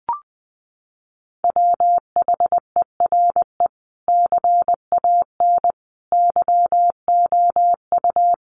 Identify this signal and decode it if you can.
Morse code, heard as a keyed tone, about 20 wpm, 700 Hz. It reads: WHERE CAN YOU
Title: Sentences from Top 500 Words 20wpm